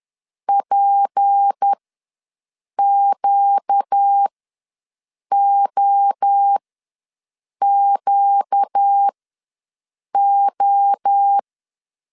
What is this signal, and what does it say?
Morse code, heard as a keyed tone, about 10 wpm, spaced as for 6 wpm, 800 Hz. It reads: PQOQO
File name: LEZIONI / 6 PKX0 / 6 P ws 10wpm 3751